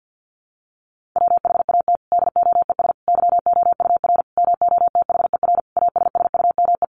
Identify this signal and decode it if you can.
Morse code, heard as a keyed tone, about 50 wpm, 700 Hz. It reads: WHAT DOES YOUR MOTHER ASSUME
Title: Sentences from Top 1500 Words 50wpm